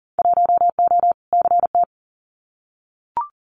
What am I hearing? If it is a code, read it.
Morse code, heard as a keyed tone, about 40 wpm, 700 Hz. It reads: 1O CT
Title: Winter Field Day Exchanges Encore 40wpm